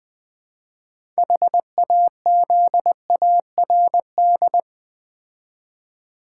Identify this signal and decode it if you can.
Morse code, heard as a keyed tone, about 20 wpm, 700 Hz. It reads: HAZARD
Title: Build A Word - 3 Letters At A Time - 6-Letter Words - 20wpm